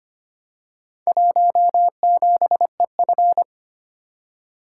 Morse code, heard as a keyed tone, about 25 wpm, 700 Hz. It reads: 17EF
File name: practice / 25wpm / Four Letter-Number Combinations - 25wpm